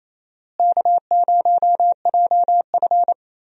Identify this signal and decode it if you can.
Morse code, heard as a keyed tone, about 28 wpm, 700 Hz. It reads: K0JF
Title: USA Call Signs - 4 Characters - Repeated 2x - 28wpm